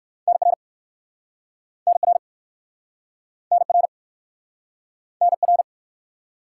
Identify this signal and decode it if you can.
Morse code, heard as a keyed tone, about 59 wpm, 700 Hz. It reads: NR NR NR NR